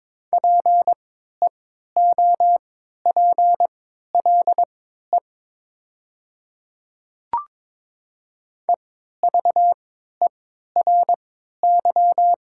Morse code, heard as a keyed tone, about 22 wpm, spaced as for 12 wpm, 700 Hz. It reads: PEOPLE EVERY
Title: 3x Character Spacing - Top 300 Words - 22wpm